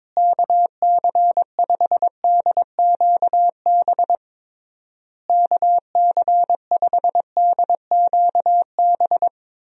Morse code, heard as a keyed tone, about 22 wpm, 700 Hz. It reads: KC5DQB KC5DQB
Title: USA Call Signs - 6 Characters - Repeated 3x - 22wpm